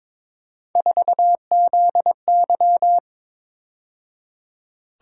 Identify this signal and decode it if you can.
Morse code, heard as a keyed tone, about 26 wpm, 700 Hz. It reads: 4ZY
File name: Speed-Racing - Three Letter-Number Combinations - 22wpm